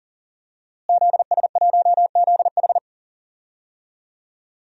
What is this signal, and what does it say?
Morse code, heard as a keyed tone, about 40 wpm, 700 Hz. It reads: ZS1ZH
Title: DXCC-100 Call Signs - 5 Characters - Repeated 2x - 40wpm